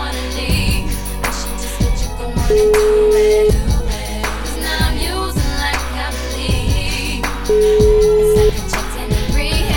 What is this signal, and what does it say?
Ringback tone (Europe, 425 Hz), with noise.